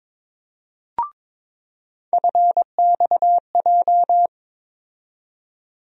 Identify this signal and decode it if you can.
Morse code, heard as a keyed tone, about 22 wpm, 700 Hz. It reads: FXJ
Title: USA Call Sign Postfixes - 3 Letters 22wpm